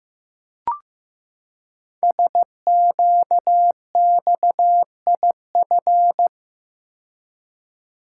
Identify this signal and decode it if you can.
Morse code, heard as a keyed tone, about 15 wpm, 700 Hz. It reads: SQXIF
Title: Code Groups - 5 Letter:Numbers - 15wpm